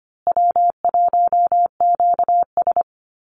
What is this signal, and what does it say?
Morse code, heard as a keyed tone, about 25 wpm, 700 Hz. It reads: W1QS
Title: Easy Call Signs Repeated 3x - 25wpm